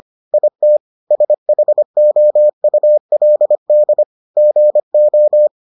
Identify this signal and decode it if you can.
Morse code, heard as a keyed tone, about 25 wpm, 600 Hz. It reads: IT SHOULD GO